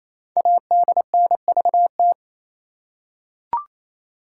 Morse code, heard as a keyed tone, about 27 wpm, 700 Hz. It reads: ADNVT
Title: Code Groups - 5 Letters - 28wpm